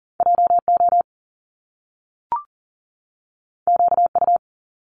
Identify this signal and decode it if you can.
Morse code, heard as a keyed tone, about 41 wpm, 700 Hz. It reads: JO QU
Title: Top Two Letter Combinations in Words 40wpm